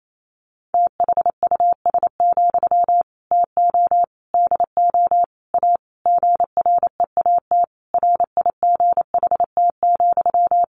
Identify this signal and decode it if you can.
Morse code, heard as a keyed tone, about 28 wpm, 690 Hz. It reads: THUS, TO DO A GREAT RIGHT,
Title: Sentences from Top 900 Words 28wpm